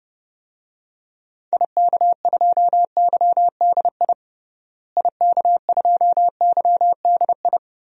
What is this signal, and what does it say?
Morse code, heard as a keyed tone, about 32 wpm, 700 Hz. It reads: IK2YDI IK2YDI
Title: DXCC-100 Call Signs - 6 Characters - Repeated 3x - 30wpm